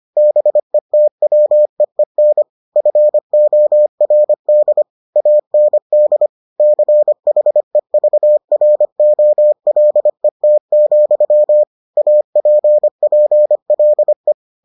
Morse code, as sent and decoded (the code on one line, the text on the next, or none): -... - .-- . -. / ..-. --- .-. -.. / .- -. -.. / -.-. .... ...- .-. --- .-.. - --..-- / .- .--. .--. .-.. .
BETWEEN FORD AND CHEVROLET, APPLE